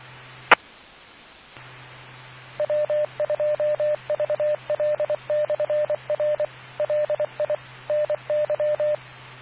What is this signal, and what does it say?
Morse code, heard as a keyed tone, about 24 wpm, 600 Hz, with radio noise. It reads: E W2VL/R LI NY